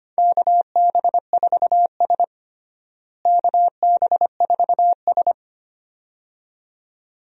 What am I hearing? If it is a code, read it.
Morse code, heard as a keyed tone, about 25 wpm, 700 Hz. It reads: KB4S KB4S
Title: USA Call Signs - 4 Characters - Repeated 2x - 25wpm